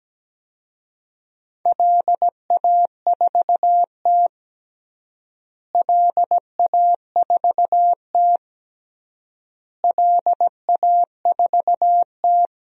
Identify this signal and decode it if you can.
Morse code, heard as a keyed tone, about 17 wpm, 700 Hz. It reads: LA4T LA4T LA4T
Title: DXCC-100 Call Signs - 4 Characters - Repeated 3x - 17wpm